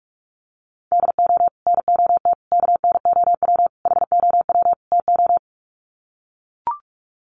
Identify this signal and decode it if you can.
Morse code, heard as a keyed tone, about 44 wpm, 700 Hz. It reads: DO NOT KNOW HOW TO
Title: Sets of 5 Words 45wpm